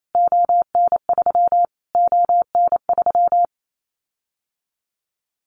Morse code, heard as a keyed tone, about 28 wpm, 700 Hz. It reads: ON3 ON3
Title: Callsign Prefixes for Top 25 DXCC - Repeated 3x - 28wpm